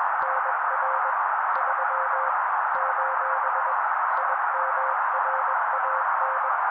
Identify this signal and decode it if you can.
Morse code, heard as a keyed tone, about 21 wpm, 530 Hz. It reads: NR 3 : IMRAN